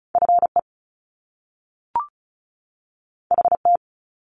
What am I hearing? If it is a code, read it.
Morse code, heard as a keyed tone, about 36 wpm, 700 Hz. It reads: FE HT
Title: Top Two Letter Combinations in Words 35wpm